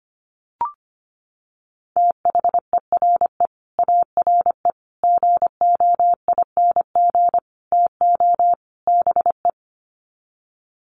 Morse code, heard as a keyed tone, about 25 wpm, 700 Hz. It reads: THERE ARE GOING TO BE
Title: Sets of 5 Words 25wpm